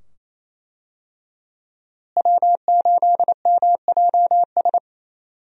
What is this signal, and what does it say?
Morse code, heard as a keyed tone, about 28 wpm, 700 Hz. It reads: W8MJS
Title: USA Call Signs - 5 Characters - Repeated 2x - 28wpm